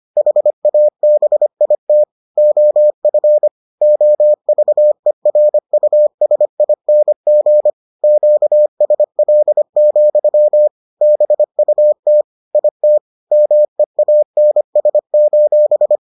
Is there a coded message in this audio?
Yes: Morse code from 0.2 to 16.0 s, heard as a keyed tone, about 25 wpm, 600 Hz. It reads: HABIT OF OVERUSING QSL, BUT IT MEANS: